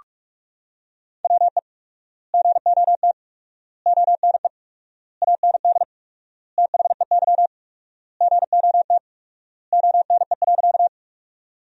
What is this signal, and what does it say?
Morse code, heard as a keyed tone, about 46 wpm, 700 Hz. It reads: WE GOT ONE AND THEY GOT ONE.
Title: 4x Word-Spacing - Sentences from Top 300 Words - 45wpm